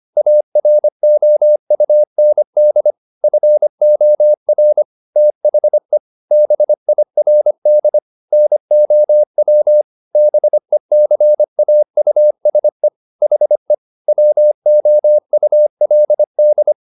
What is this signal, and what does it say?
Morse code, heard as a keyed tone, about 25 wpm, 600 Hz. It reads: AROUND FOR THE BIRD NOW BECAUSE HE WOULD
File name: the old man and the sea part 1 25wpm